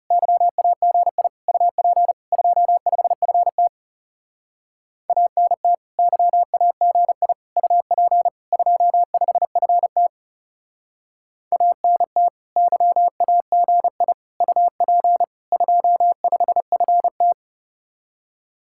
Morse code, heard as a keyed tone, about 37 wpm, 700 Hz. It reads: YAGI UP 25FT ANT YAGI UP 25FT ANT YAGI UP 25FT
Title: Speed-Racing Common QSO Phrases Encore - 30wpm